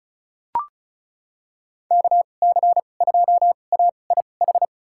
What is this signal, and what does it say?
Morse code, heard as a keyed tone, about 36 wpm, spaced as for 25 wpm, 700 Hz. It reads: KC2AIH
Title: USA Call Signs - 6 Characters - 2x Character Spacing - 35wpm